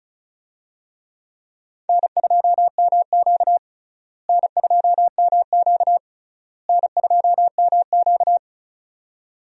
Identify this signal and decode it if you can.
Morse code, heard as a keyed tone, about 36 wpm, 700 Hz. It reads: N2MQ N2MQ N2MQ
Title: USA Call Signs - 4 Characters - Repeated 3x - 35wpm